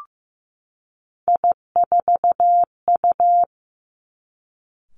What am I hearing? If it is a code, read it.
Morse code, heard as a keyed tone, about 16 wpm, 700 Hz. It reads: I4U
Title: Three Challenging Letter-Number Combinations 15wpm